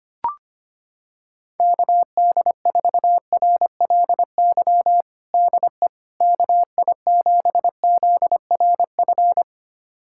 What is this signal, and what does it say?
Morse code, heard as a keyed tone, about 25 wpm, 700 Hz. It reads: KD4RLY DE KI7ZRF
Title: USA Callsign DE Callsign 25wpm